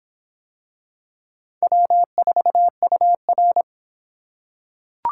Morse code, heard as a keyed tone, about 26 wpm, 700 Hz. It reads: W4UR